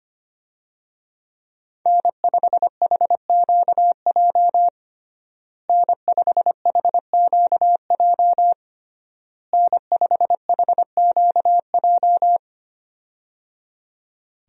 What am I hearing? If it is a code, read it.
Morse code, heard as a keyed tone, about 25 wpm, 700 Hz. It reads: N5HQJ N5HQJ N5HQJ
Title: USA Call Signs - 5 Characters - Repeated 3x - 25wpm